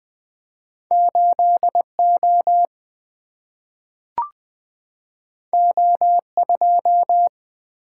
Morse code, heard as a keyed tone, about 20 wpm, 700 Hz. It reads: 8O O2
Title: Single-Letter Single-Number 20wpm